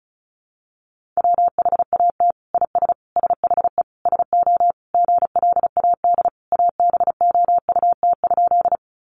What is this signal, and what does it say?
Morse code, heard as a keyed tone, about 34 wpm, 700 Hz. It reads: WHAT IS SHE SO GLAD ABOUT?
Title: Sentences from Top 800 Words 35wpm